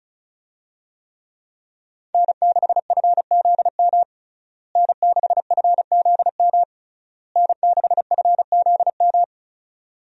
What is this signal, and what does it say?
Morse code, heard as a keyed tone, about 34 wpm, 700 Hz. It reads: N6FZM N6FZM N6FZM
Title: USA Call Signs - 5 Characters - Repeated 3x - 35wpm